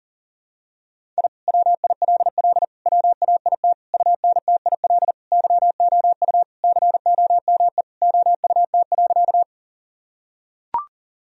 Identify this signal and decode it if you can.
Morse code, heard as a keyed tone, about 40 wpm, 700 Hz. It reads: I WILL WAIT UNTIL YOU COME OUT.
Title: Sentences from Top 1000 Words - Review Part 1 - 40wpm